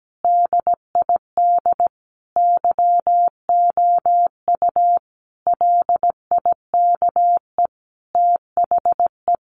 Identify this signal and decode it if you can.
Morse code, heard as a keyed tone, about 17 wpm, 700 Hz. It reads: DID YOU LIKE THE
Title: Sentences from Top 1300 Words 17wpm